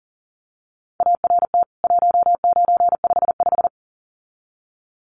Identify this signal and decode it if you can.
Morse code, heard as a keyed tone, about 37 wpm, 700 Hz. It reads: ART 1955
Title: CWT Exchange 40wpm